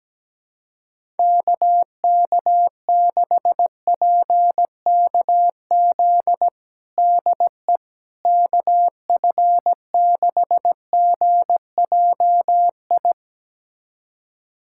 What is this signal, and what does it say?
Morse code, heard as a keyed tone, about 17 wpm, 700 Hz. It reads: KK6PKZ DE KF6GJI
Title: USA Callsign DE Callsign 17wpm